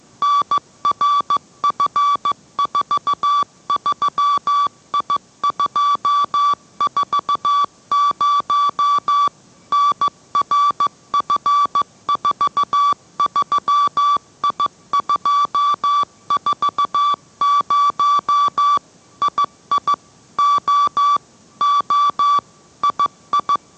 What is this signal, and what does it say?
Morse code, heard as a keyed tone, about 15 wpm, 1150 Hz. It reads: NRF43I240 NRF43I240 II O O II